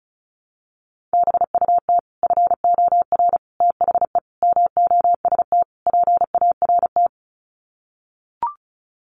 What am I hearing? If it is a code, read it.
Morse code, heard as a keyed tone, about 34 wpm, 700 Hz. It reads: BUT FOR THE MOST PART
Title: Sets of 5 Words Encore 35wpm